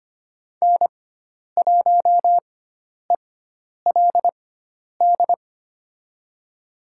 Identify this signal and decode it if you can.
Morse code, heard as a keyed tone, about 25 wpm, spaced as for 10 wpm, 700 Hz. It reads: N1ELD
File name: Lesson 16g - Letter c - Call Signs 25wpm